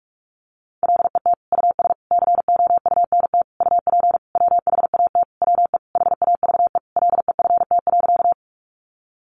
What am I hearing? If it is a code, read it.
Morse code, heard as a keyed tone, about 44 wpm, 700 Hz. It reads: LET US COUNT UP WHAT WE HAVE LEFT.
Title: Sentences from Top 1500 Words 45wpm